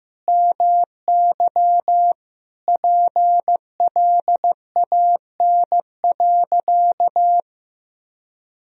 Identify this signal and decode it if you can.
Morse code, heard as a keyed tone, about 15 wpm, 700 Hz. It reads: MY PLAN.
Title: Sentences from Top 500 Words and Abbreviations 15wpm